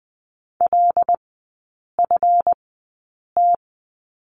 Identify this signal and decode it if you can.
Morse code, heard as a keyed tone, about 20 wpm, spaced as for 8 wpm, 700 Hz. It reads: LFT
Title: 138 Lesson 23e - Letter f - Three Characters 20wpm